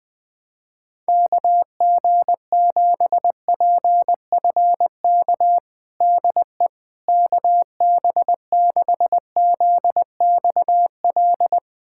Morse code, heard as a keyed tone, about 21 wpm, 700 Hz. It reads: KG7PFK DE KB6ZXL
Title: USA Callsign DE Callsign - 6 Characters - 20wpm